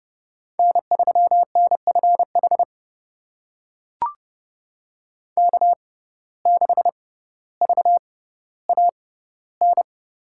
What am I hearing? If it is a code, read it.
Morse code, heard as a keyed tone, about 32 wpm, spaced as for 16 wpm, 700 Hz. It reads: N3NFH K6VAN
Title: USA Call Signs - 5 Characters - 6x Character Spacing - 30wpm